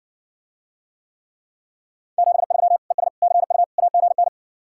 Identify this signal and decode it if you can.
Morse code, heard as a keyed tone, about 60 wpm, 700 Hz. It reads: 73 ES CU AGN